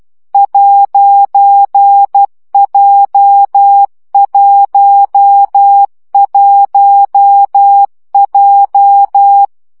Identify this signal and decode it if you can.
Morse code, heard as a keyed tone, about 12 wpm, 800 Hz. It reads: 'J11J
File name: CW gyakorlas j 1 aposztrof 10 baud